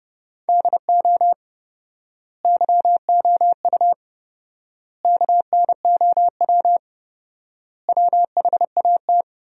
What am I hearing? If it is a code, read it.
Morse code, heard as a keyed tone, about 30 wpm, 700 Hz. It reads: DO YOU KNOW WHAT
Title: 4x Word-Spacing - Sentences from Top 100 Words - 30wpm